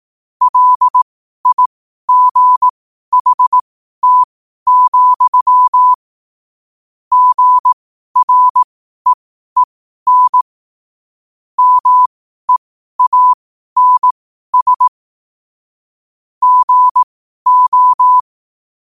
Morse code, heard as a keyed tone, about 18 wpm, spaced as for 12 wpm, 1000 Hz. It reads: LIGHT, GREEN MEANS GO
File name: qotdfiles / 12wpm